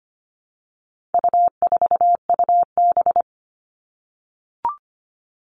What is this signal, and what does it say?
Morse code, heard as a keyed tone, about 25 wpm, 700 Hz. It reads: U4UB
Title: All Cut Numbers - 4-Digit 25wpm